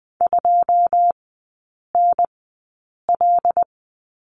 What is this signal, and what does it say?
Morse code, heard as a keyed tone, about 20 wpm, spaced as for 8 wpm, 700 Hz. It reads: 2NL
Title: 081 Lesson 14e - Number 2 - Three Characters 20wpm